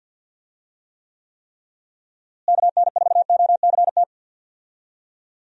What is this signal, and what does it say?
Morse code, heard as a keyed tone, about 52 wpm, 700 Hz. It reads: KN4OCT
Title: USA Call Signs - 6 Characters - Repeated 3x - 50wpm